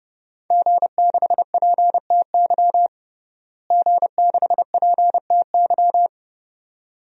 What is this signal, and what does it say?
Morse code, heard as a keyed tone, about 30 wpm, 700 Hz. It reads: G6PTY G6PTY
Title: DXCC-100 Call Signs - 5 Characters - Repeated 2x - 30wpm